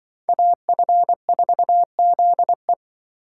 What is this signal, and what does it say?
Morse code, heard as a keyed tone, about 25 wpm, 700 Hz. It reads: AF4ZE
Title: Speed Racing - USA 2x2 Call Signs 20wpm